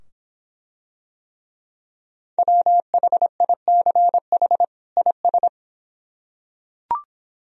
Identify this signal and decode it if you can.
Morse code, heard as a keyed tone, about 24 wpm, 700 Hz. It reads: WHICH IS